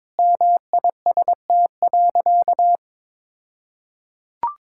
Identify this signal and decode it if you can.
Morse code, heard as a keyed tone, about 22 wpm, 700 Hz. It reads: MIST.
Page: morse-code-ninja-practice-22wpm